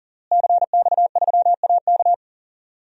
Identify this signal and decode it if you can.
Morse code, heard as a keyed tone, about 40 wpm, 700 Hz. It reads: CX3AK
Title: DXCC-100 Call Signs - 5 Characters - Repeated 2x - 40wpm